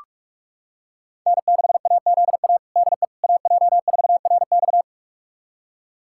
Morse code, heard as a keyed tone, about 42 wpm, 700 Hz. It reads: N6AZA DE AJ4RX
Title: USA Callsign DE Callsign - 5 Characters - 45wpm